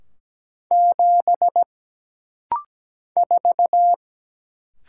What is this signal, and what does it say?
Morse code, heard as a keyed tone, about 17 wpm, 700 Hz. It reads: 7 4
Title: Single Letter-Number Alt - Rapid-Fire 17wpm